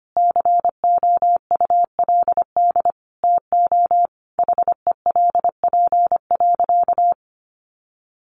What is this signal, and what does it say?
Morse code, heard as a keyed tone, about 25 wpm, 700 Hz. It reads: COULD TO HELP.
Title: Sentences from Top 1100 Words 25wpm